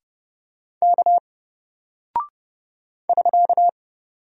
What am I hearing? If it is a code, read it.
Morse code, heard as a keyed tone, about 30 wpm, 700 Hz. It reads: K <SK>
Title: Single Character (Letter:Number:Punctuation:Prosign) 30wpm